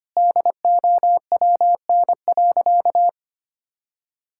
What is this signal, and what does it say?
Morse code, heard as a keyed tone, about 25 wpm, 700 Hz. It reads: DOWN.
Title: Sentences from Top 1600 Words 25wpm